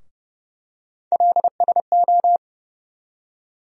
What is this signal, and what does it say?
Morse code, heard as a keyed tone, about 30 wpm, 700 Hz. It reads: LSO
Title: USA Call Sign Postfixes - 3 Letters 30wpm